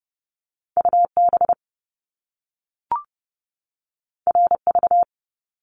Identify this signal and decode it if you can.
Morse code, heard as a keyed tone, about 30 wpm, 700 Hz. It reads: U B R V
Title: Top Two Letter Combinations in Words 30wpm